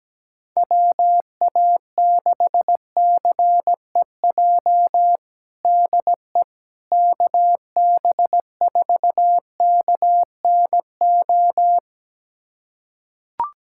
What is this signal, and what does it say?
Morse code, heard as a keyed tone, about 17 wpm, 700 Hz. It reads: WA6CEJ DE KB4KNO
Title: USA Callsign DE Callsign - 6 Characters - 17wpm